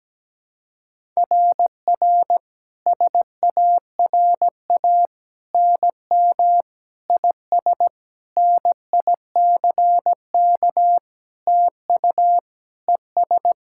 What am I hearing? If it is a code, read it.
Morse code, heard as a keyed tone, about 17 wpm, 700 Hz. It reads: RR SARA NM IS NICK TU ES
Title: SOTA QSO Practice 17wpm